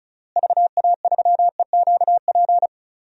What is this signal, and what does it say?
Morse code, heard as a keyed tone, about 36 wpm, 700 Hz. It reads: VA3EQP
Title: DXCC-100 Call Signs - 6 Characters - Repeated 3x - 35wpm